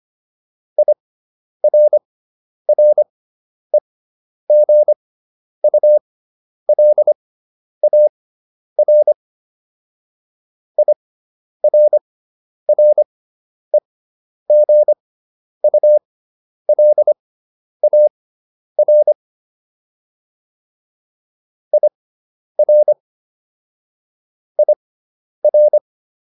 Morse code, heard as a keyed tone, about 25 wpm, spaced as for 10 wpm, 600 Hz. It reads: IRREGULAR IRREGULAR IR IR